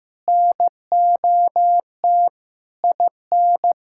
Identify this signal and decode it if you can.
Morse code, heard as a keyed tone, about 15 wpm, 700 Hz. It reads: NOT IN
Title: Sentences from Top 1600 Words 15wpm